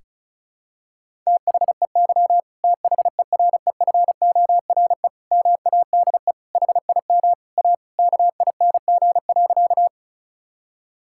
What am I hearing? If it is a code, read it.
Morse code, heard as a keyed tone, about 36 wpm, 700 Hz. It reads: THEY THEREFORE MADE HIM A KING.
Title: Sentences from Top 1000 Words - Review Part 5 - 35wpm